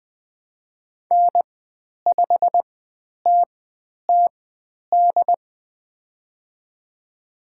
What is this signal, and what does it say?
Morse code, heard as a keyed tone, about 20 wpm, spaced as for 10 wpm, 700 Hz. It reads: N5TTD
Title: 095 Lesson 16g - Letter c - Call Signs 20wpm